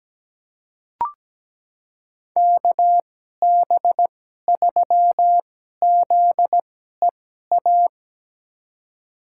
Morse code, heard as a keyed tone, about 17 wpm, spaced as for 12 wpm, 700 Hz. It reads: KB3ZEA
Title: USA Call Signs - 2x Character Spacing - 17wpm